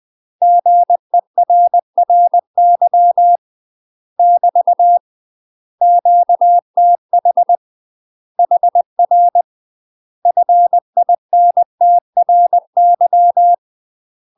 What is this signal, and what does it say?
Morse code, heard as a keyed tone, about 20 wpm, 700 Hz. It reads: GERRY <BT> QTH HR FINTRY
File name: UK MASTER QSOs TRAINING / 20WPM / UK MASTER QSOs 0073